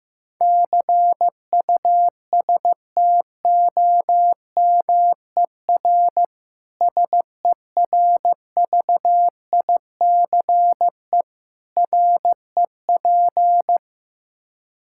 Morse code, heard as a keyed tone, about 15 wpm, 700 Hz. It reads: CUSTOMER SERVICE REP
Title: Practice QSO - Exchange 3 - More Info - 15wpm